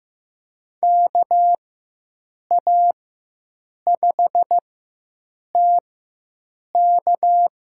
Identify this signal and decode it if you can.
Morse code, heard as a keyed tone, about 15 wpm, spaced as for 7 wpm, 700 Hz. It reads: KA5TK
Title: USA Call Signs - 4x Character Spacing - 15wpm